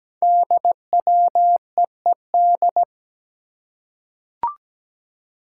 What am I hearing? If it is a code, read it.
Morse code, heard as a keyed tone, about 17 wpm, 700 Hz. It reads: DWEED
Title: Code Groups - 5 Letter:Numbers - 17wpm